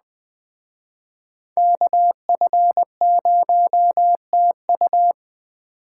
Morse code, heard as a keyed tone, about 20 wpm, 700 Hz. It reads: KF0TU
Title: Speed Racing - USA 2x2 Call Signs 17wpm